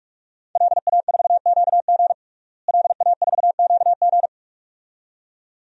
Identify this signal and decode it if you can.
Morse code, heard as a keyed tone, about 46 wpm, 700 Hz. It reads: LA4QG LA4QG
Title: DXCC-100 Call Signs - 5 Characters - Repeated 2x - 45wpm